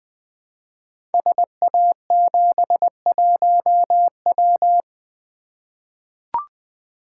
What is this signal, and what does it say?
Morse code, heard as a keyed tone, about 20 wpm, 700 Hz. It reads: SA71W
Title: Code Groups - 5 Letter:Numbers - 20wpm